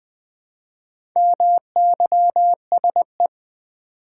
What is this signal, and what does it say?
Morse code, heard as a keyed tone, about 20 wpm, 690 Hz. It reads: MYSE
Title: Build A Word - 2 Letters At A Time - 6-Letter Words - 20wpm